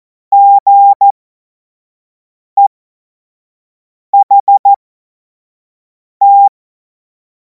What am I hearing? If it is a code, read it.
Morse code, heard as a keyed tone, about 14 wpm, spaced as for 5 wpm, 800 Hz. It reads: GEHT
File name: Morsekurs / Witze / Witz 19. Mai 2026 25 F70